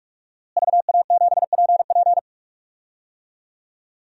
Morse code, heard as a keyed tone, about 46 wpm, 700 Hz. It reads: VA7PP
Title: DXCC-100 Call Signs - 5 Characters - Repeated 3x - 45wpm